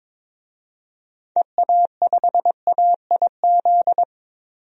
Morse code, heard as a keyed tone, about 21 wpm, 700 Hz. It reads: EA5AIZ
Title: DXCC-100 Call Signs - 6 Characters - 22wpm